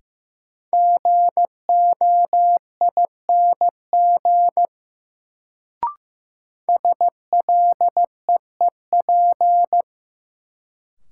Morse code, heard as a keyed tone, about 15 wpm, 700 Hz. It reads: GOING SLEEP
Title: Sentences from Top 500 Words - New Word Drill - 15wpm